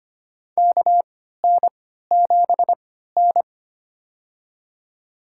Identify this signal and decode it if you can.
Morse code, heard as a keyed tone, about 25 wpm, spaced as for 14 wpm, 700 Hz. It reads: KN7N
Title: USA Call Signs - 4 Characters - 3x Character Spacing - 25wpm